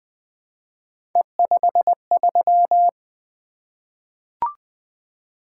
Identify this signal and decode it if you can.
Morse code, heard as a keyed tone, about 20 wpm, 700 Hz. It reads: E53
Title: Three Letter-Number Combinations - 20wpm